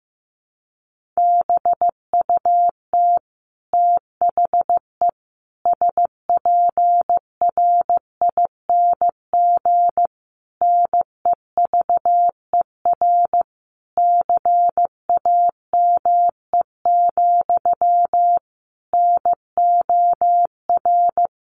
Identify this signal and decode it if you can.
Morse code, heard as a keyed tone, about 15 wpm, 700 Hz. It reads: BUT THE SPRING NEVER CAME, NOR
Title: Sentences from Top 1500 Words 15wpm